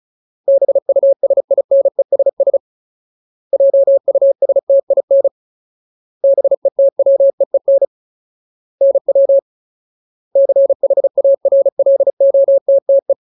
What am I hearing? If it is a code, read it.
Morse code, heard as a keyed tone, about 34 wpm, 550 Hz. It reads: BUSINESS JUSTIN BETWEEN NW CHARLOTTE
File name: Ultimate Word List / 35WPM Ultimate